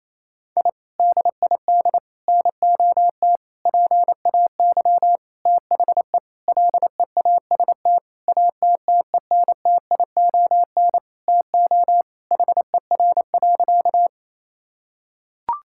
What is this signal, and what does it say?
Morse code, heard as a keyed tone, about 28 wpm, 700 Hz. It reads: I DID NOT PAY THE LEAST ATTENTION TO HER.
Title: Sentences from Top 1000 Words - Review Part 6 - 28wpm